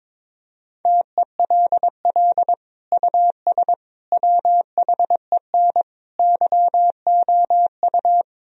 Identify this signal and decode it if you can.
Morse code, heard as a keyed tone, about 22 wpm, 700 Hz. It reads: TELL US WHEN YOU
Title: Sentences from Top 1000 Words - Review Part 2 - 22wpm